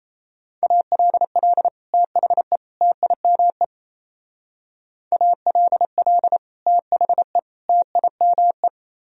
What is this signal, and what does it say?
Morse code, heard as a keyed tone, about 35 wpm, 700 Hz. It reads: ALL THE TIME ALL THE TIME